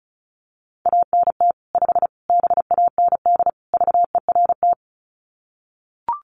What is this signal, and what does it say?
Morse code, heard as a keyed tone, about 34 wpm, 700 Hz. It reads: ANT 5 BAND VERT
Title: Common QSO Phrases 35wpm